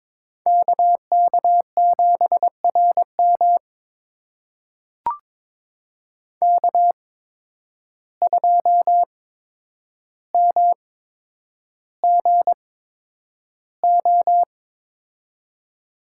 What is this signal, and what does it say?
Morse code, heard as a keyed tone, about 22 wpm, spaced as for 9 wpm, 700 Hz. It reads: KK7RM K2MGO
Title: USA Call Signs - 5 Characters - 8x Character Spacing - 22wpm